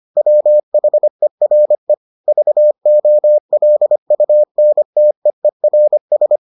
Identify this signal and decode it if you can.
Morse code, heard as a keyed tone, about 25 wpm, 600 Hz. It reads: WHERE VOLUNTEERS